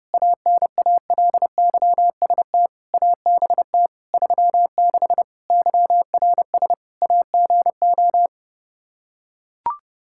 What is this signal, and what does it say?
Morse code, heard as a keyed tone, about 30 wpm, 700 Hz. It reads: ANALYST ABT 36 YRS AGO
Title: Practice QSO - Exchange 3 - More Info - 30wpm